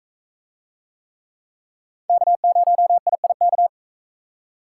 Morse code, heard as a keyed tone, about 44 wpm, 700 Hz. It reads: K0IIK